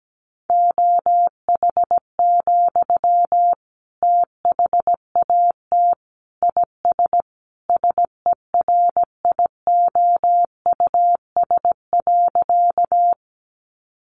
Morse code, heard as a keyed tone, about 17 wpm, 700 Hz. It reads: OH, THAT IS SERIOUS.
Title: Sentences from Top 1100 Words 17wpm